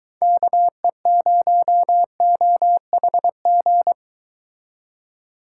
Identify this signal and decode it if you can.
Morse code, heard as a keyed tone, about 23 wpm, 700 Hz. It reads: KE0OHG